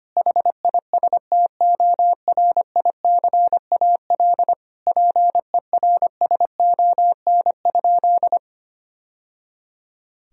Morse code, heard as a keyed tone, about 25 wpm, 700 Hz. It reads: HISTORICAL PERSON?
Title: Sentences from Top 1000 Words - Review Part 4 - 25wpm